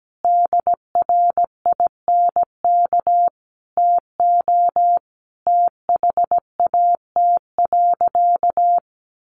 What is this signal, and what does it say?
Morse code, heard as a keyed tone, about 17 wpm, 700 Hz. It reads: DRINK TO THAT.
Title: Sentences from Top 1500 Words 17wpm